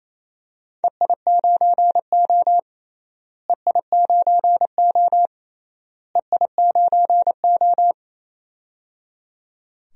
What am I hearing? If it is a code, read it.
Morse code, heard as a keyed tone, about 28 wpm, 700 Hz. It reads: EI9O EI9O EI9O
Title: DXCC-100 Call Signs - 4 Characters - Repeated 3x - 28wpm